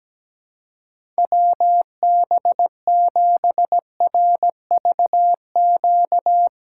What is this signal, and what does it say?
Morse code, heard as a keyed tone, about 17 wpm, 700 Hz. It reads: WB7RVQ
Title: USA Call Signs - 6 Characters - 4x Character Spacing - 17wpm